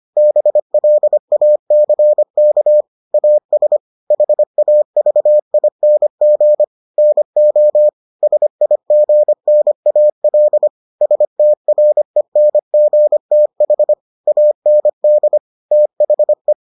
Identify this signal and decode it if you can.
Morse code, heard as a keyed tone, about 25 wpm, 600 Hz. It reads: BLACK AS HAVING NO SIGNAL STRENGTH AND THE